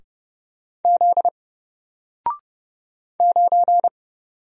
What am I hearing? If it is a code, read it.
Morse code, heard as a keyed tone, about 30 wpm, 700 Hz. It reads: Z 9
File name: Single-Letter-Number-30wpm